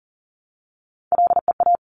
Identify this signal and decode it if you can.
Morse code, heard as a keyed tone, about 38 wpm, 700 Hz. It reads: LEA
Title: Top Three Letter Combinations in Words 40wpm